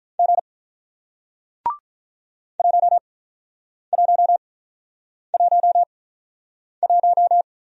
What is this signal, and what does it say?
Morse code, heard as a keyed tone, about 48 wpm, spaced as for 9 wpm, 700 Hz. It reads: G 1111